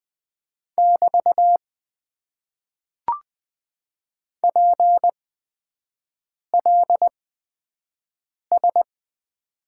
Morse code, heard as a keyed tone, about 20 wpm, spaced as for 5 wpm, 700 Hz. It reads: <BT> PLS
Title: Top 90 QSO Elements - 8x Character Spacing - 20wpm